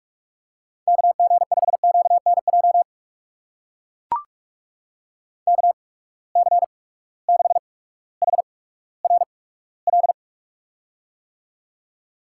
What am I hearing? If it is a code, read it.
Morse code, heard as a keyed tone, about 44 wpm, spaced as for 19 wpm, 700 Hz. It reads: KG5QNJ KC6HRL
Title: USA Call Signs - 6 Characters - 8x Character Spacing - 45wpm